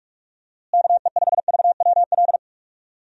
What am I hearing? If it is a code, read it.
Morse code, heard as a keyed tone, about 46 wpm, 700 Hz. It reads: KE5VWL
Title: USA Call Signs - 6 Characters - Repeated 2x - 45wpm